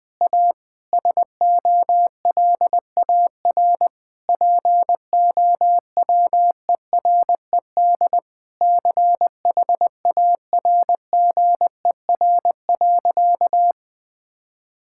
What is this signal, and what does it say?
Morse code, heard as a keyed tone, about 20 wpm, 700 Hz. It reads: A SOLAR POWERED CHARGER.
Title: Sentences with 6-Word Phrases - 20wpm